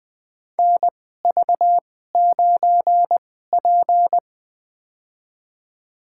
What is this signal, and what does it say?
Morse code, heard as a keyed tone, about 20 wpm, spaced as for 15 wpm, 700 Hz. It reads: NV9P
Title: USA Call Signs - 4 Characters - 2x Character Spacing - 20wpm